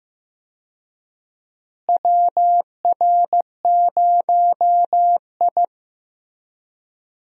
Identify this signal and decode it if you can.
Morse code, heard as a keyed tone, about 15 wpm, 700 Hz. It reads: WR0I
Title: USA Call Signs - 4 Characters - Repeated 3x - 15wpm